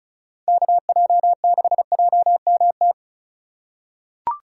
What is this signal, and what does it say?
Morse code, heard as a keyed tone, about 36 wpm, 700 Hz. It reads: KJ6JMT
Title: USA Call Signs - 6 Characters - 3x Character Spacing - 35wpm